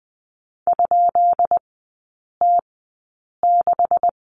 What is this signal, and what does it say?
Morse code, heard as a keyed tone, about 20 wpm, spaced as for 8 wpm, 700 Hz. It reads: ?T6
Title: 133 Lesson 22e - Question Mark - Three Characters 20wpm